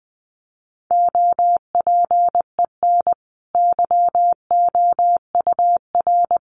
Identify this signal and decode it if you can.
Morse code, heard as a keyed tone, about 20 wpm, 700 Hz. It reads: OPEN YOUR
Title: Sentences from Top 1500 Words 20wpm